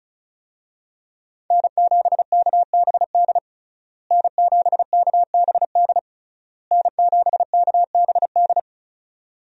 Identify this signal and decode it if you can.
Morse code, heard as a keyed tone, about 36 wpm, 700 Hz. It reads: N7KBD N7KBD N7KBD
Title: USA Call Signs - 5 Characters - Repeated 3x - 35wpm